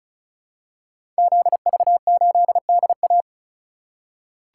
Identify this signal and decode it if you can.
Morse code, heard as a keyed tone, about 36 wpm, 700 Hz. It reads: ZV8DA
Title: Code Groups - 5 Letter:Numbers - 35wpm